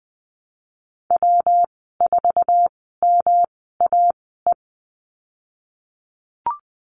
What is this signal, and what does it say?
Morse code, heard as a keyed tone, about 20 wpm, spaced as for 15 wpm, 700 Hz. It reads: W4MAE
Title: 128 Lesson 21g - Number 6 - Call Signs 20wpm